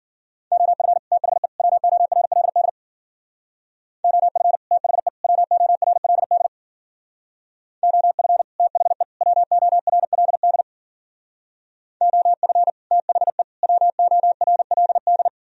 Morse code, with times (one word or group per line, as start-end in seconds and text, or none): 0.5-1.0 OF
1.1-1.5 THE
1.6-2.7 WORLD
4.0-4.6 OF
4.7-5.1 THE
5.2-6.5 WORLD
7.8-8.4 OF
8.6-9.0 THE
9.2-10.6 WORLD
12.0-12.7 OF
12.9-13.4 THE
13.6-15.3 WORLD